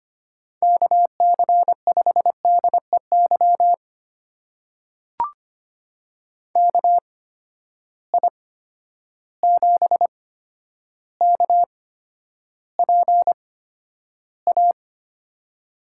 Morse code, heard as a keyed tone, about 25 wpm, spaced as for 11 wpm, 700 Hz. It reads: KC5DEY KI7KPA
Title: USA Call Signs - 6 Characters - 8x Character Spacing - 25wpm